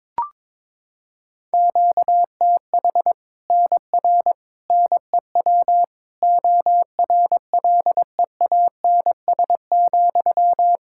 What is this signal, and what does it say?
Morse code, heard as a keyed tone, about 22 wpm, 700 Hz. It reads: QTH NR NEW ORLEANS,
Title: K1USN SST QSO Practice 22wpm